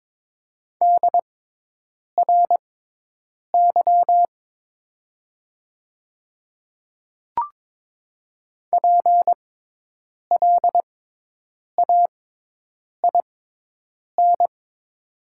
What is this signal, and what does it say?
Morse code, heard as a keyed tone, about 22 wpm, spaced as for 8 wpm, 700 Hz. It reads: DRY PLAIN
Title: 6x Character Spacing - Top 500 Words - 22wpm